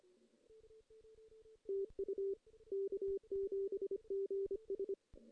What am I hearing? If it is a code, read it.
Morse code, heard as a keyed tone, about 25 wpm, 410 Hz. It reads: TU K7GS
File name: 2016 JAN CW NAQP / 2016 Jan naqp k5ot - 0-2